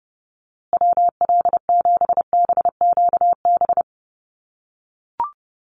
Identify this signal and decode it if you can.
Morse code, heard as a keyed tone, about 30 wpm, 700 Hz. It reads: WL7BQB
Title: USA Call Signs Repeated 3x 30wpm